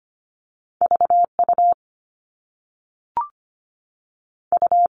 Morse code, heard as a keyed tone, about 25 wpm, 700 Hz. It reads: VU U
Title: All Cut Numbers - 2-Digit 25wpm